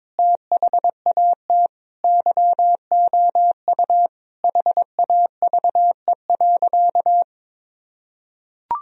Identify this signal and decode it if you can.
Morse code, heard as a keyed tone, about 22 wpm, 700 Hz. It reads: THAT YOU HAVE.
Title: Sentences from Top 1000 Words - Review Part 4 - 22wpm